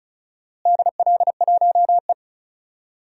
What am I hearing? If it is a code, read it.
Morse code, heard as a keyed tone, about 34 wpm, 700 Hz. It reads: DL1E
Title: DXCC-100 Call Signs - 4 Characters - Repeated 2x - 35wpm